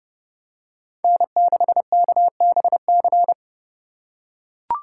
Morse code, heard as a keyed tone, about 29 wpm, 700 Hz. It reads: N6KBC